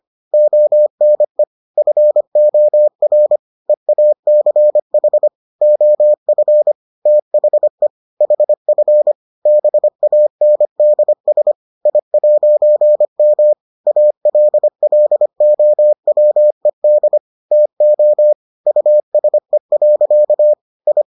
Morse code, heard as a keyed tone, about 25 wpm, 600 Hz. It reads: ONE FOR EACH OF THE HF BANDS I'M ALLOWED TO USE. I